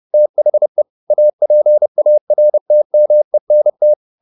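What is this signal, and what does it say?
Morse code, heard as a keyed tone, about 30 wpm, 600 Hz. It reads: THE APARTMENT
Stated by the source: Morse code at 29 wpm, sending Ken Follett's novel Never in English, from CW med Gnister.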